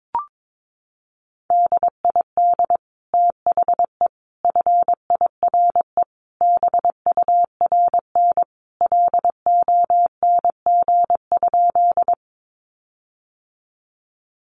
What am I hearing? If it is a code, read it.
Morse code, heard as a keyed tone, about 22 wpm, 700 Hz. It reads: DID THE FIRE BURN LONG?
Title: Sentences from Top 1300 Words 22wpm